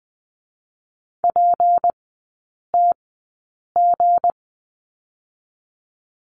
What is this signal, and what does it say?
Morse code, heard as a keyed tone, about 20 wpm, spaced as for 8 wpm, 700 Hz. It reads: PTG
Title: 178 Lesson 29e - Slash - Three Characters 20wpm